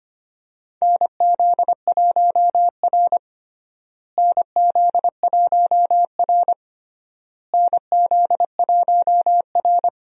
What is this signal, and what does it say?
Morse code, heard as a keyed tone, about 25 wpm, 700 Hz. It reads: NZ1R NZ1R NZ1R
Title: USA Call Signs - 4 Characters - Repeated 3x - 25wpm